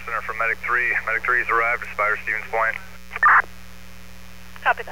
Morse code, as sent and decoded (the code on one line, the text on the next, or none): none